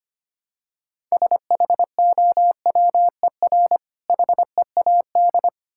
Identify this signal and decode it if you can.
Morse code, heard as a keyed tone, about 27 wpm, 700 Hz. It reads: SHOWER HEAD
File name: Easy Words - Speed-Racing with Hints - 25wpm